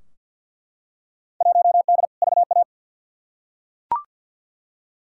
Morse code, heard as a keyed tone, about 50 wpm, 700 Hz. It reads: 1D VA
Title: Field Day Exchanges - 50wpm